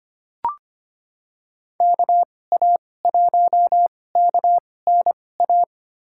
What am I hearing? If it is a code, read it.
Morse code, heard as a keyed tone, about 25 wpm, spaced as for 18 wpm, 700 Hz. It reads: KA1KNA
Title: USA Call Signs - 6 Characters - 2x Character Spacing - 25wpm